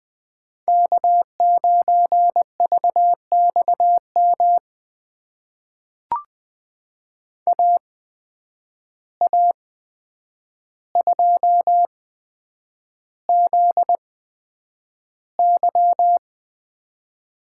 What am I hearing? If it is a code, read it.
Morse code, heard as a keyed tone, about 20 wpm, spaced as for 9 wpm, 700 Hz. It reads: K9VXM AA2ZY
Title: USA Call Signs - 5 Characters - 8x Character Spacing - 20wpm